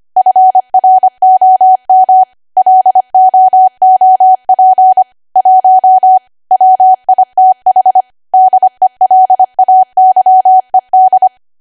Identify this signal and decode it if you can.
Morse code, heard as a keyed tone, about 25 wpm, 750 Hz. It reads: FROM LOOP 1 WITH DELAYED
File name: Archive / 25 WPM / 25WPM